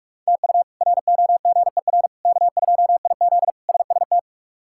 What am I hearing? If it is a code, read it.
Morse code, heard as a keyed tone, about 46 wpm, 700 Hz. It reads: TU ROGER K2IZ SST
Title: K1USN SST QSO Practice - 45wpm